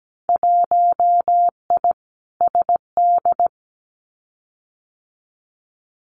Morse code, heard as a keyed tone, about 17 wpm, 700 Hz. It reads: 1I SD
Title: Winter Field Day Exchanges 17wpm